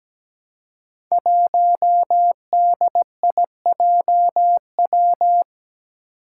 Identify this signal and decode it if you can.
Morse code, heard as a keyed tone, about 17 wpm, 700 Hz. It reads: 1DIJW
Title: Five Challenging Letter-Number Combinations 17wpm